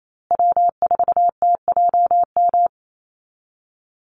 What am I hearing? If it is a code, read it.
Morse code, heard as a keyed tone, about 28 wpm, 700 Hz. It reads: W4TJM
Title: USA Call Signs Repeated 2x 28wpm